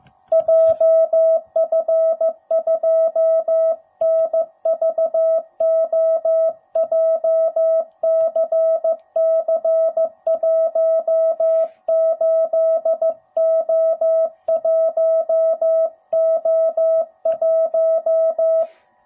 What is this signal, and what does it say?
Morse code, heard as a keyed tone, about 14 wpm, 650 Hz. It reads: JF2NVO JCC18O1O1
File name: jh8cbh / 2021cw / cw05b